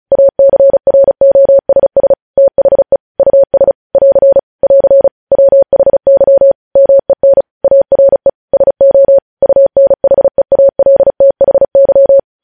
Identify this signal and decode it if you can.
Morse code, heard as a keyed tone, about 36 wpm, 550 Hz. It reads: ACROSS THE US <AR> <AR> WHY MEN ARE SO UNHEALTHY
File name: podcast / News headlines-1x-35x35